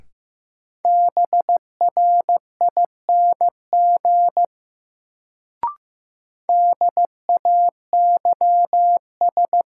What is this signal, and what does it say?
Morse code, heard as a keyed tone, about 15 wpm, 700 Hz. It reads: BRING DAYS
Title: Sets of 5 Words Encore - New Word Drill - 15wpm